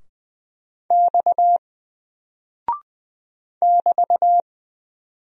Morse code, heard as a keyed tone, about 20 wpm, 700 Hz. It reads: X <BT>
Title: Single Character (Letter:Number:Punctuation:Prosign) Alt 20wpm